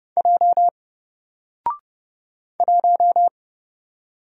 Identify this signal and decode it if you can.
Morse code, heard as a keyed tone, about 30 wpm, 700 Hz. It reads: J 1
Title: Single Letter-Number Alt 30wpm